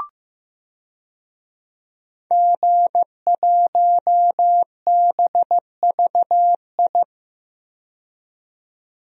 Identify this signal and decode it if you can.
Morse code, heard as a keyed tone, about 15 wpm, 700 Hz. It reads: G1BVI
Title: DXCC-100 Call Signs - 5 Characters - 15wpm